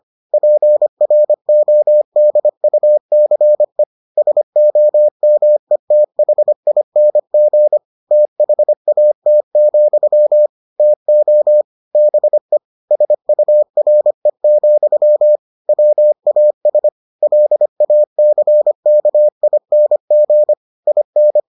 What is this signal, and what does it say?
Morse code, heard as a keyed tone, about 25 wpm, 600 Hz. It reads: PRODUCE SOMETHING THAT, TO BE SURE, WAS LACKING IN